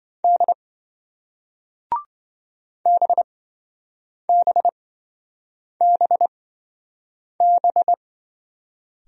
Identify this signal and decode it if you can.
Morse code, heard as a keyed tone, about 27 wpm, spaced as for 7 wpm, 700 Hz. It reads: D BBBB